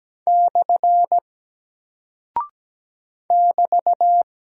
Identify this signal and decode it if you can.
Morse code, heard as a keyed tone, about 17 wpm, 700 Hz. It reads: / <BT>
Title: Single Character (Letter:Number:Punctuation:Prosign) Alt 17wpm